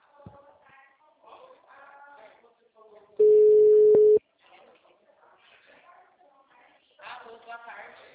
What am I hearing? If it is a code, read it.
Ringback tone (Europe, 425 Hz), with speech.